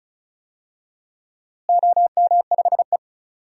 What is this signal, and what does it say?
Morse code, heard as a keyed tone, about 36 wpm, 700 Hz. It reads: OM5E
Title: DXCC-100 Call Signs - 4 Characters - Repeated 2x - 35wpm